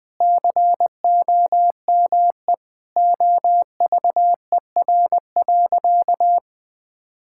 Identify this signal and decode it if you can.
Morse code, heard as a keyed tone, about 20 wpm, 700 Hz. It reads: COME OVER.
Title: Sentences with 4-Word Phrases - Spoken Beginning - 20wpm